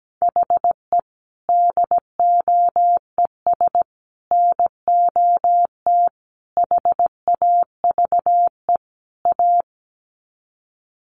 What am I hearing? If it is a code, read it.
Morse code, heard as a keyed tone, about 17 wpm, 700 Hz. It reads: HE DOES NOT HAVE A
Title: Sets of 5 Words Encore 17wpm